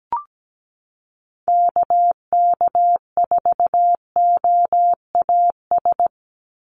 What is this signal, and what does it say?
Morse code, heard as a keyed tone, about 17 wpm, 700 Hz. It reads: KK4OAS
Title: USA Call Signs 17wpm